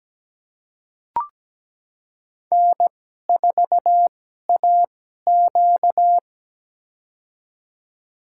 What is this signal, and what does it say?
Morse code, heard as a keyed tone, about 17 wpm, spaced as for 12 wpm, 700 Hz. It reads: N4AQ
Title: USA Call Signs - 2x Character Spacing - 17wpm